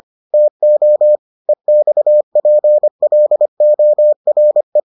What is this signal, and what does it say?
Morse code, heard as a keyed tone, about 25 wpm, 600 Hz. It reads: TO EXPLORE